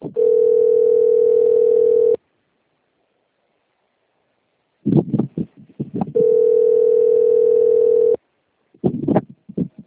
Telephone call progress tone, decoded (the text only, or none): ringback tone